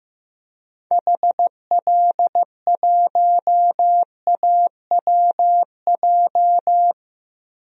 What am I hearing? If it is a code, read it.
Morse code, heard as a keyed tone, about 15 wpm, 700 Hz. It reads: HL1AWJ
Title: DXCC-100 Call Signs - 6 Characters - Repeated 2x - 15wpm